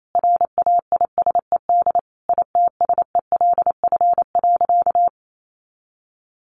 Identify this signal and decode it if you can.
Morse code, heard as a keyed tone, about 28 wpm, 700 Hz. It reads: RAISED ITSELF.
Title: A Sound Of Thunder 28wpm